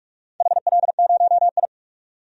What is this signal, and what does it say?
Morse code, heard as a keyed tone, about 48 wpm, 700 Hz. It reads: HL0I
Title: DXCC-100 Call Signs - 4 Characters - 45wpm